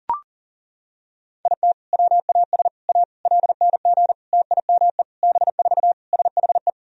Morse code, heard as a keyed tone, about 40 wpm, 700 Hz. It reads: IT WAS A LNG TIME B4 SHE
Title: Sentences from Top 500 Words and Abbreviations 40wpm